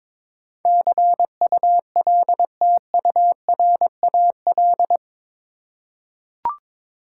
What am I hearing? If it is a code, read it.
Morse code, heard as a keyed tone, about 22 wpm, 700 Hz. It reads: CULTURAL
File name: practice / 22wpm / Eight Letter Words - 22wpm